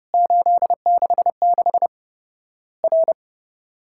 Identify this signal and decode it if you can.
Morse code, heard as a keyed tone, about 30 wpm, 700 Hz. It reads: 866 R
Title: Worked All Europe DX Contest - QTC Exchange - 30wpm